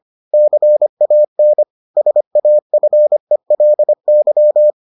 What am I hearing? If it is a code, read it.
Morse code, heard as a keyed tone, about 25 wpm, 600 Hz. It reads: CAN SAFELY